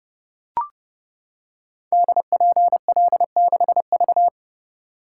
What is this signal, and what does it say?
Morse code, heard as a keyed tone, about 30 wpm, 700 Hz. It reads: DPL6V
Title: Code Groups - 5 Letter:Numbers - 30wpm